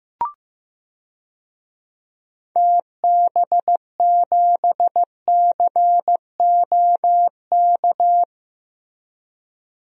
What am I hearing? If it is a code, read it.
Morse code, heard as a keyed tone, about 15 wpm, 700 Hz. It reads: TB7COK
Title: DXCC-100 Call Signs - 6 Characters - Repeated 3x - 15wpm